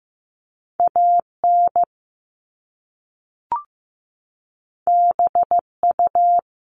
Morse code, heard as a keyed tone, about 15 wpm, 700 Hz. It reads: AN BU
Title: Top Two Letter Combinations in Words 15wpm